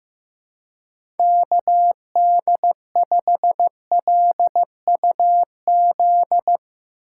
Morse code, heard as a keyed tone, about 16 wpm, 700 Hz. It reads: KD5LUZ
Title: Speed Racing - USA 2x3 Call Signs 15wpm